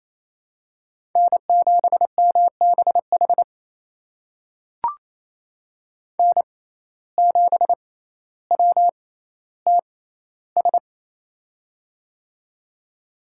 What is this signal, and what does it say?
Morse code, heard as a keyed tone, about 28 wpm, spaced as for 14 wpm, 700 Hz. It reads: N7MBH N7WTS
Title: USA Call Signs - 5 Characters - 6x Character Spacing - 28wpm